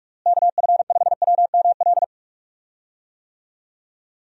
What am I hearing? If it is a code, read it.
Morse code, heard as a keyed tone, about 41 wpm, 700 Hz. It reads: KF5WML
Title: USA Call Signs - 6 Characters - Repeated 3x - 45wpm